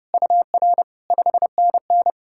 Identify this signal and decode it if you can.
Morse code, heard as a keyed tone, about 36 wpm, 700 Hz. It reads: UR 5NN
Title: Speed-Racing Common QSO Phrases Encore - 30wpm